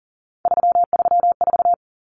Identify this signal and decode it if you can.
Morse code, heard as a keyed tone, about 40 wpm, 700 Hz. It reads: 334
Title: Numbers - 3-Digit 40wpm